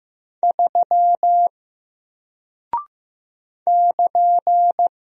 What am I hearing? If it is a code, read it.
Morse code, heard as a keyed tone, about 15 wpm, 700 Hz. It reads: 3 <KN>
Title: Single Character (Letter:Number:Punctuation:Prosign) 15wpm